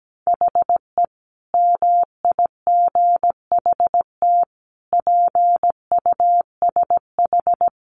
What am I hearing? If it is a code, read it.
Morse code, heard as a keyed tone, about 17 wpm, 700 Hz. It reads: HE MIGHT PUSH
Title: Sentences from Top 1300 Words 17wpm